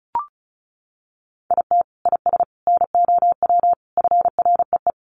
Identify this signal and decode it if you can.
Morse code, heard as a keyed tone, about 34 wpm, 700 Hz. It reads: IT IS NOW FREE
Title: Sentences from Top 1300 Words 35wpm